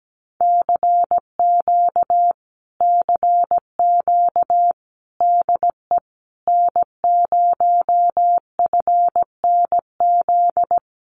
Morse code, heard as a keyed tone, about 17 wpm, 700 Hz. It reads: CQ CQ DE N0FNZ
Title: Common QSO Phrases Encore 17wpm